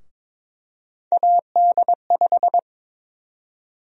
Morse code, heard as a keyed tone, about 22 wpm, 700 Hz. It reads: AD5
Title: USA Call Sign Prefixes - Repeated 2x - 22wpm